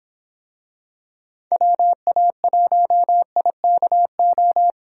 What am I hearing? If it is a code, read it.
Morse code, heard as a keyed tone, about 26 wpm, 700 Hz. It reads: WA1IKO